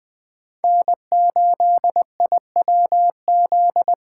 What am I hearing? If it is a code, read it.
Morse code, heard as a keyed tone, about 20 wpm, 700 Hz. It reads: N8IWZ